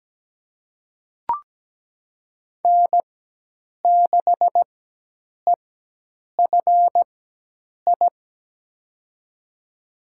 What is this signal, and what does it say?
Morse code, heard as a keyed tone, about 17 wpm, spaced as for 8 wpm, 700 Hz. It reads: N6EFI
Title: USA Call Signs - 5 Characters - 4x Character Spacing - 17wpm